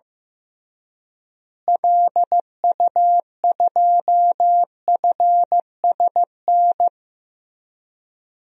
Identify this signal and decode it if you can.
Morse code, heard as a keyed tone, about 15 wpm, 700 Hz. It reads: LU2FSN
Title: DXCC-100 Call Signs - 6 Characters - Repeated 3x - 15wpm